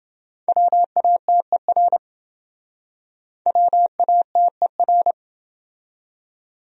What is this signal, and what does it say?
Morse code, heard as a keyed tone, about 29 wpm, 700 Hz. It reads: WATER WATER